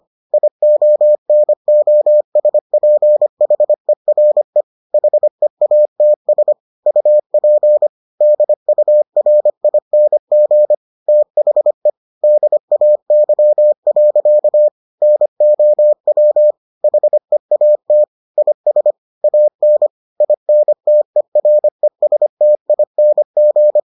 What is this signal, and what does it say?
Morse code, heard as a keyed tone, about 25 wpm, 600 Hz. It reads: IONOSPHERE HEATS UP DURING THE DAY. NOW HEAT IS AN INTERESTING